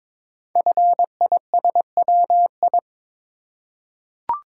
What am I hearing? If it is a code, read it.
Morse code, heard as a keyed tone, about 22 wpm, 700 Hz. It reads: FISWI
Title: Code Groups - 5 Letters - 22wpm